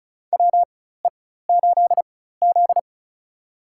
Morse code, heard as a keyed tone, about 36 wpm, spaced as for 16 wpm, 700 Hz. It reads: WE8Z
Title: USA Call Signs - 4x Character Spacing - 35wpm